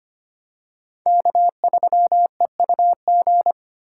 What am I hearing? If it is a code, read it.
Morse code, heard as a keyed tone, about 25 wpm, 700 Hz. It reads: K3EUG
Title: USA Call Signs - 5 Characters - 2x Character Spacing - 25wpm